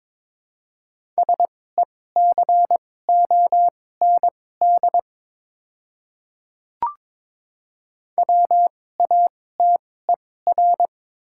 Morse code, heard as a keyed tone, about 22 wpm, spaced as for 16 wpm, 700 Hz. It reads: SECOND WATER
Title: 2x Character Spacing - Top 300 Words - 22wpm